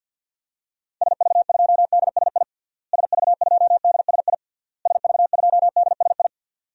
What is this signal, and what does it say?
Morse code, heard as a keyed tone, about 48 wpm, 700 Hz. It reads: SV2DSI SV2DSI SV2DSI
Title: DXCC-100 Call Signs - 6 Characters - Repeated 3x - 50wpm